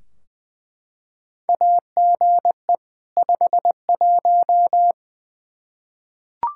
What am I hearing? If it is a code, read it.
Morse code, heard as a keyed tone, about 20 wpm, 700 Hz. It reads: AGE 51
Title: Practice QSO - Exchange 3 - More Info - 20wpm